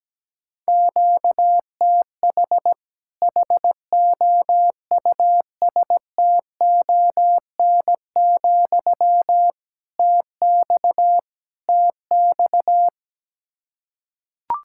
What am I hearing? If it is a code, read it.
Morse code, heard as a keyed tone, about 17 wpm, 700 Hz. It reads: QTH HOUSTON, TX TX
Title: Practice QSO - Exchange 1 - Beginning - 17wpm